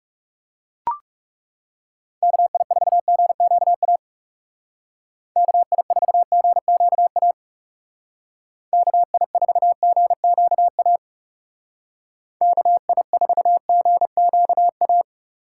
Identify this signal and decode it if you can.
Morse code, heard as a keyed tone, about 36 wpm, 700 Hz. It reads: KI4GQA KI4GQA KI4GQA KI4GQA
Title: Speed Racing - USA 2x3 Call Signs 30wpm